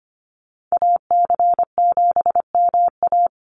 Morse code, heard as a keyed tone, about 25 wpm, 700 Hz. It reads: AC7MA
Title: NA SOTA Activator Callsigns 25wpm